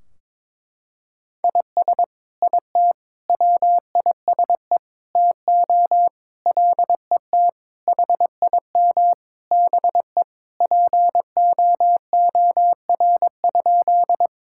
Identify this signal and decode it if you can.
Morse code, heard as a keyed tone, about 21 wpm, 700 Hz. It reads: IS IT WISE TO LET HIM BE POOR?
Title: Sentences from Top 1600 Words 22wpm